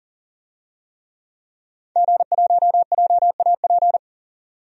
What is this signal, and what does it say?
Morse code, heard as a keyed tone, about 40 wpm, 700 Hz. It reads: G1JAP
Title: DXCC-100 Call Signs - 5 Characters - 40wpm